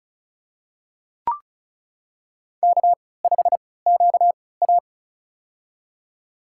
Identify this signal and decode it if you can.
Morse code, heard as a keyed tone, about 36 wpm, spaced as for 20 wpm, 690 Hz. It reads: K5QA
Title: USA Call Signs - 4 Characters - 3x Character Spacing - 35wpm